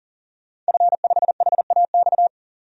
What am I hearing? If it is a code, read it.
Morse code, heard as a keyed tone, about 40 wpm, 700 Hz. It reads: F5HAX
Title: DXCC-100 Call Signs - 5 Characters - 40wpm